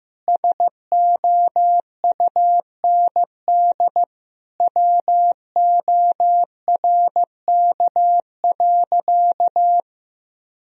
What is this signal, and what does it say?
Morse code, heard as a keyed tone, about 15 wpm, 700 Hz. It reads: SOUND WORK.
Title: Sentences from Top 1000 Words - Review Part 2 - 15wpm